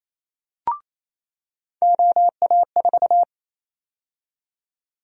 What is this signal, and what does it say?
Morse code, heard as a keyed tone, about 28 wpm, 700 Hz. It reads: OA4
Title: USA Call Sign Prefixes - 28wpm